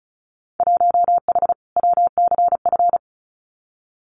Morse code, heard as a keyed tone, about 34 wpm, 700 Hz. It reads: 1H WCF
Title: Winter Field Day Exchanges 35wpm